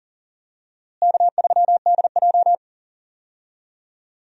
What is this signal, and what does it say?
Morse code, heard as a keyed tone, about 40 wpm, 700 Hz. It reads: K3DJ
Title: USA Call Signs - 4 Characters - Repeated 2x - 40wpm